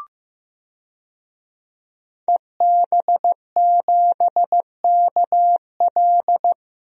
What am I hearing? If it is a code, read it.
Morse code, heard as a keyed tone, about 15 wpm, 700 Hz. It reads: EB7KL
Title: DXCC-100 Call Signs - 5 Characters - Repeated 2x - 15wpm